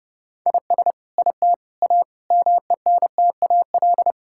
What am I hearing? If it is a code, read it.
Morse code, heard as a keyed tone, about 30 wpm, 700 Hz. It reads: IS IT A MENTAL
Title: Sentences from Top 1000 Words - Review Part 6 - 30wpm